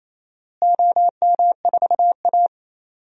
Morse code, heard as a keyed tone, about 27 wpm, 700 Hz. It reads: OM4A
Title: DXCC-100 Call Signs - 4 Characters - Repeated 2x - 28wpm